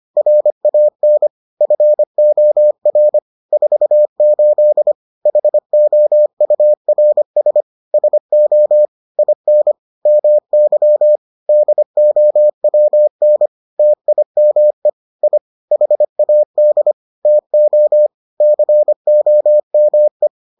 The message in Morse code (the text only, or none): RAN FOR 48 HOURS SO IN MY DOWN TIME I HAD TO COME